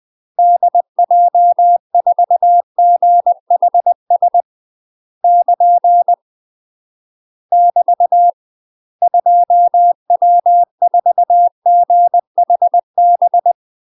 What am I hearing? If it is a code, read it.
Morse code, heard as a keyed tone, about 20 wpm, 700 Hz. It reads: DJ4GHS <KN> <BT> 2W4GHB
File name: UK MASTER QSOs TRAINING / 20WPM / UK MASTER QSOs 0055